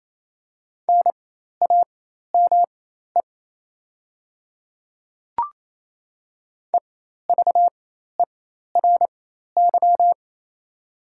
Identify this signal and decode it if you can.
Morse code, heard as a keyed tone, about 27 wpm, spaced as for 13 wpm, 690 Hz. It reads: NAME EVERY
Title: 4x Character Spacing - Top 300 Words - 28wpm